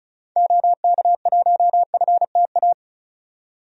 Morse code, heard as a keyed tone, about 34 wpm, 700 Hz. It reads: OK1FTA
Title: DXCC-100 Call Signs - 6 Characters - 35wpm